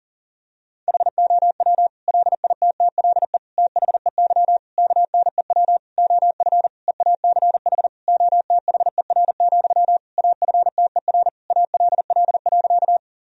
Morse code, heard as a keyed tone, about 37 wpm, 700 Hz. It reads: HOW LITTLE THEY KNEW OF EACH OTHER, AFTER ALL.
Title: Sentences from Top 1000 Words - Review Part 1 - 40wpm